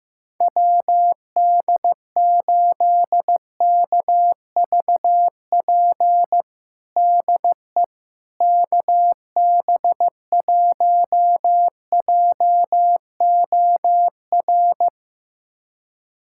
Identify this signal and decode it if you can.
Morse code, heard as a keyed tone, about 15 wpm, 700 Hz. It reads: WD8KVP DE KB1JOR
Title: USA Callsign DE Callsign - 6 Characters - 15wpm